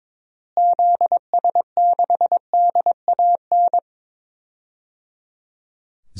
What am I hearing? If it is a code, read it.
Morse code, heard as a keyed tone, about 22 wpm, 700 Hz. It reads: ZS6DAN
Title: DXCC-100 Call Signs - 6 Characters - Repeated 3x - 22wpm